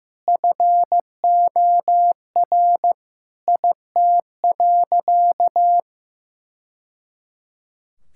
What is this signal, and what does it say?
Morse code, heard as a keyed tone, about 15 wpm, 700 Hz. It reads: FOR IT.
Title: Sentences from Top 1000 Words - Review Part 6 - 15wpm